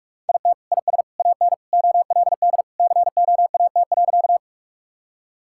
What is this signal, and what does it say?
Morse code, heard as a keyed tone, about 47 wpm, 700 Hz. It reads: IT IS AN OLD COAT.
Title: Sentences from Top 1600 Words 45wpm